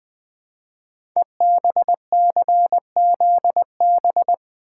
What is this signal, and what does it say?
Morse code, heard as a keyed tone, about 20 wpm, 700 Hz. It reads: EBCZB
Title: Code Groups - 5 Letters - 20wpm